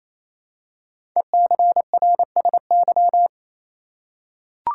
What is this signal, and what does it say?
Morse code, heard as a keyed tone, about 29 wpm, 700 Hz. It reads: ECRSY
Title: Code Groups - 5 Letters - 28wpm